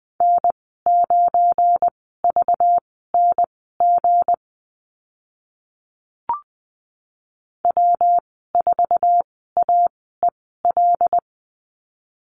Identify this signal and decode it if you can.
Morse code, heard as a keyed tone, about 20 wpm, spaced as for 15 wpm, 700 Hz. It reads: N9VNG W4AEL
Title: 219 Lesson 35g - Number 0 - Call Signs 20wpm